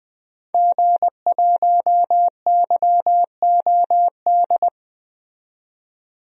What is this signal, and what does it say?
Morse code, heard as a keyed tone, about 20 wpm, 700 Hz. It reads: G1YOD
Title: DXCC-100 Call Signs - 5 Characters - 20wpm